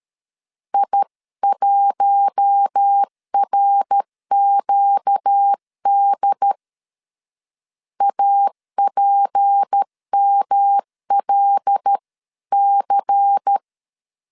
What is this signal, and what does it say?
Morse code, heard as a keyed tone, about 13 wpm, 800 Hz. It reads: I1RQD APMLC
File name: CW / DET / 1 6 1500C S C 12wpm 3533